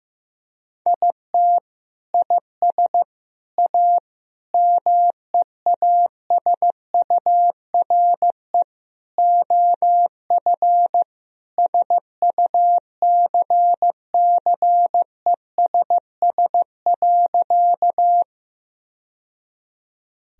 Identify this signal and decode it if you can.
Morse code, heard as a keyed tone, about 15 wpm, 700 Hz. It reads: IT IS A MEASURE OF SUCCESS.
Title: Sentences from Top 1000 Words - Review Part 2 - 15wpm